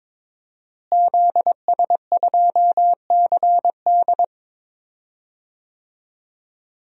Morse code, heard as a keyed tone, about 22 wpm, 700 Hz. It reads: ZS2CD
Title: DXCC-100 Call Signs - 5 Characters - Repeated 2x - 22wpm